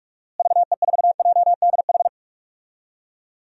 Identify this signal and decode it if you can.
Morse code, heard as a keyed tone, about 46 wpm, 700 Hz. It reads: VE4JDH
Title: DXCC-100 Call Signs - 6 Characters - Repeated 2x - 45wpm